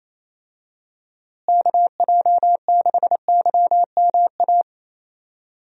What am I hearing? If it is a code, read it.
Morse code, heard as a keyed tone, about 28 wpm, 700 Hz. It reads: KJ6YMA
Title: USA Call Signs - 6 Characters - Repeated 3x - 28wpm